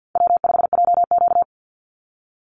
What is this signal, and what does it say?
Morse code, heard as a keyed tone, about 50 wpm, 700 Hz. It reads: W5JQ
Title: Call Signs - 50wpm